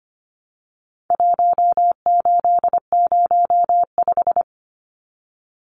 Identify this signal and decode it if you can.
Morse code, heard as a keyed tone, about 25 wpm, 700 Hz. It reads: 1805
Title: Numbers - Contest Style - 4-Digit 25wpm